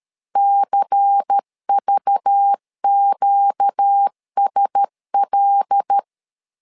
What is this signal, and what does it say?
Morse code, heard as a keyed tone, about 13 wpm, 800 Hz. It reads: CVQSL
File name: LEZIONI / 5 LBVC / 5 LBVC 12wpm 1955